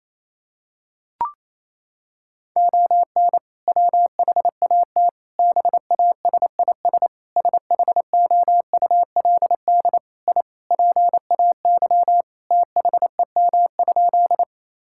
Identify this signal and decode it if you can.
Morse code, heard as a keyed tone, about 28 wpm, 700 Hz. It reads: ON WHAT BASIS SHOULD I PAY THEM?
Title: Sentences from Top 1000 Words - Review Part 5 - 28wpm